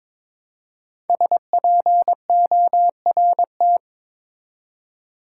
Morse code, heard as a keyed tone, about 22 wpm, 700 Hz. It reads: SPORT
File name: 5 Letter Words - First 3 Letters Given - 22wpm